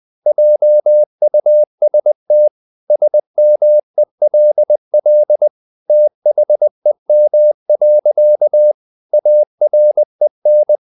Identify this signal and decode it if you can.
Morse code, heard as a keyed tone, about 20 wpm, 600 Hz. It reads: JUST SMELL THEM. AREN
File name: the old man and the sea / the old man and the sea part 1 20wpm